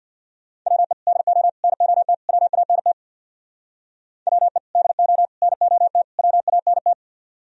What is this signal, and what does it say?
Morse code, heard as a keyed tone, about 62 wpm, 700 Hz. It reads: WE DO NOT WANT WE DO NOT WANT
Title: Speed-Racing - Sets of 4 Words 50wpm